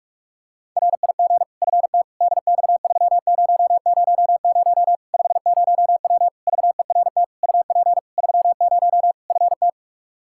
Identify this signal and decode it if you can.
Morse code, heard as a keyed tone, about 46 wpm, 700 Hz. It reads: RIG FT DX3000 50W VERT UP 30 FT
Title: Practice QSO - Exchange 2 - Info - 45wpm